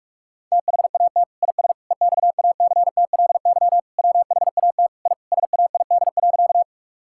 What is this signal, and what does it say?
Morse code, heard as a keyed tone, about 46 wpm, 700 Hz. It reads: THAT IS EXACTLY WHAT I SAID.
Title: Sentences from Top 1000 Words - Review Part 6 - 45wpm